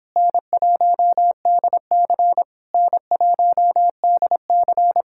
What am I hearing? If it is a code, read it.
Morse code, heard as a keyed tone, about 26 wpm, 700 Hz. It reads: N1DC N1DC